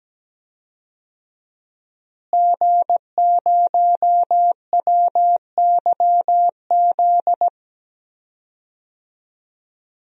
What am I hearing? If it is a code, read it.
Morse code, heard as a keyed tone, about 17 wpm, 700 Hz. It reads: G0WYZ
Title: DXCC-100 Call Signs - 5 Characters - Repeated 3x - 17wpm